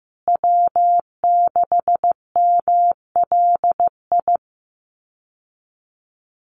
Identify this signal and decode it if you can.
Morse code, heard as a keyed tone, about 15 wpm, 700 Hz. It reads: W6MLI
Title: USA Call Signs Repeated 2x 15wpm